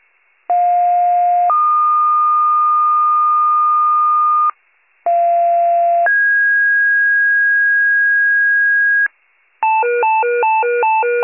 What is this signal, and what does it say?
Two-tone page, heard as a one-second tone, then a three-second tone, about 710 then 1220 Hz, about 690 then 1650 Hz, with noise. It is heard from 0.5 to 9.1 s.